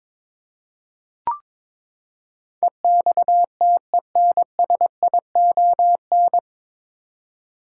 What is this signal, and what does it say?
Morse code, heard as a keyed tone, about 21 wpm, 700 Hz. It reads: EXTENSION
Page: Morse Code Ninja practice 22wpm